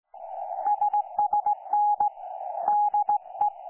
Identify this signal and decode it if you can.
Morse code, heard as a keyed tone, about 17 wpm, 830 Hz. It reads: SSN DE
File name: 09 38Z 18088.0kHz k3fef darrel